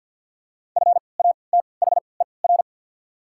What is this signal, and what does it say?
Morse code, heard as a keyed tone, about 48 wpm, spaced as for 28 wpm, 690 Hz. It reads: FATHER
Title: 3x Character Spacing - Top 300 Words - 50wpm